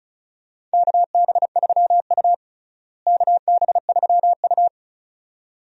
Morse code, heard as a keyed tone, about 34 wpm, 700 Hz. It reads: KB3U KB3U
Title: USA Call Signs - 4 Characters - Repeated 2x - 35wpm